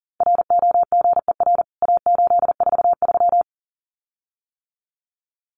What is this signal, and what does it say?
Morse code, heard as a keyed tone, about 40 wpm, 700 Hz. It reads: ROGER A843
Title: CWT Exchange 40wpm